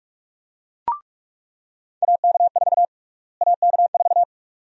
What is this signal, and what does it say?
Morse code, heard as a keyed tone, about 42 wpm, 700 Hz. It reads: AK4 AK4
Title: USA Call Sign Prefixes - Repeated 2x - 45wpm